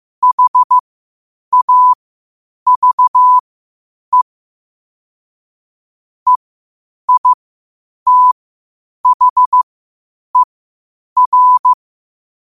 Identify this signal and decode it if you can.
Morse code, heard as a keyed tone, about 15 wpm, spaced as for 8 wpm, 1000 Hz. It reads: HAVE EITHER
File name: qotdfiles / 8wpm